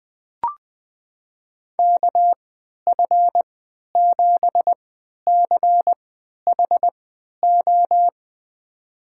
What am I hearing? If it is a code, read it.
Morse code, heard as a keyed tone, about 21 wpm, spaced as for 11 wpm, 700 Hz. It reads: KF7CHO
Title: USA Call Signs - 6 Characters - 3x Character Spacing - 20wpm